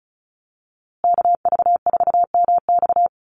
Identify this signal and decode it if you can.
Morse code, heard as a keyed tone, about 34 wpm, 700 Hz. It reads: KV4MX
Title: USA Call Signs Repeated 3x 35wpm